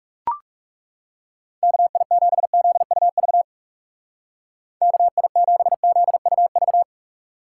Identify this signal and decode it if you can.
Morse code, heard as a keyed tone, about 39 wpm, 700 Hz. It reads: KI7ZUV KI7ZUV